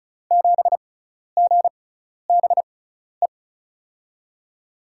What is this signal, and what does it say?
Morse code, heard as a keyed tone, about 36 wpm, spaced as for 12 wpm, 700 Hz. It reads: 7GBE
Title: USA Call Signs - 6 Characters - 6x Character Spacing - 35wpm